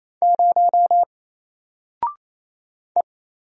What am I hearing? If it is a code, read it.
Morse code, heard as a keyed tone, about 28 wpm, 700 Hz. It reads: 0 E
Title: Single Letter-Number Alt 28wpm